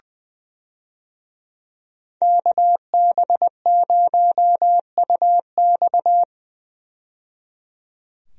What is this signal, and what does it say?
Morse code, heard as a keyed tone, about 20 wpm, 700 Hz. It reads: KB0UX
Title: USA Call Signs - 5 Characters - 20wpm